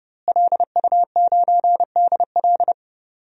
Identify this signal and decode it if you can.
Morse code, heard as a keyed tone, about 30 wpm, 700 Hz. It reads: LU9DL
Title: DXCC-100 Call Signs - 5 Characters - Repeated 2x - 30wpm